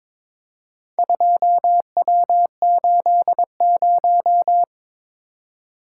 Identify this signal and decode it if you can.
Morse code, heard as a keyed tone, about 22 wpm, 700 Hz. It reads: 2W80
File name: practice / 22wpm / Four Letter-Number Combinations - 22wpm